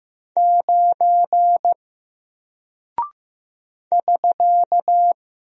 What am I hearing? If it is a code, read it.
Morse code, heard as a keyed tone, about 15 wpm, 700 Hz. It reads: 9 <SK>
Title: Single Character (Letter:Number:Punctuation:Prosign) 15wpm